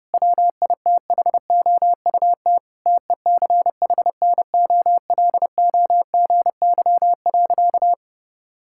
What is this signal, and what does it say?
Morse code, heard as a keyed tone, about 30 wpm, 700 Hz. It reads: WITHOUT TECHNOLOGY.
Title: Sentences from Top 1000 Words - Review Part 5 - 30wpm